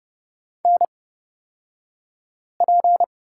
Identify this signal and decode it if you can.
Morse code, heard as a keyed tone, about 30 wpm, spaced as for 5 wpm, 700 Hz. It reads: NP
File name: Lesson 37d - Letter x - Two Characters 30wpm